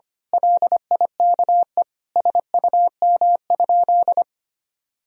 Morse code, heard as a keyed tone, about 25 wpm, 700 Hz. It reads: LIKE SUM?
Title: Sentences from Top 100 Words and Abbreviations 25wpm